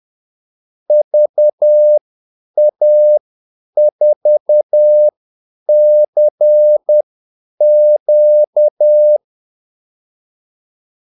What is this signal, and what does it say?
Morse code, heard as a keyed tone, about 10 wpm, spaced as for 8 wpm, 600 Hz. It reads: VA4CQ
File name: Callsigns-10-8wpm